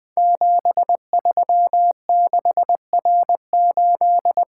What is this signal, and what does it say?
Morse code, heard as a keyed tone, about 20 wpm, 700 Hz. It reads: 736R8
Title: Code Groups - 5 Letter:Numbers - 20wpm